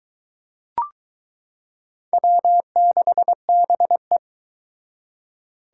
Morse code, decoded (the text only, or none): W6BE